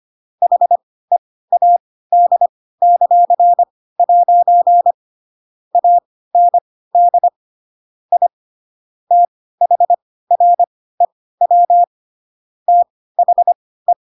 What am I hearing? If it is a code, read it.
Morse code, heard as a keyed tone, about 25 wpm, spaced as for 16 wpm, 700 Hz. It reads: HEAD;' AND I THREW THE
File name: Pinocchio / Pinocchio ch7 10 25 ik1hge